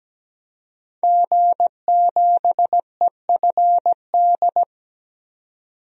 Morse code, heard as a keyed tone, about 17 wpm, 700 Hz. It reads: G7EFD
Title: DXCC-100 Call Signs - 5 Characters - 17wpm